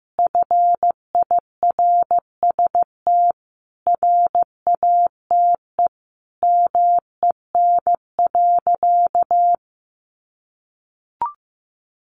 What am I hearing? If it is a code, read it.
Morse code, heard as a keyed tone, about 15 wpm, 700 Hz. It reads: FIRST RATE MEN.
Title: Sentences from Top 1300 Words 15wpm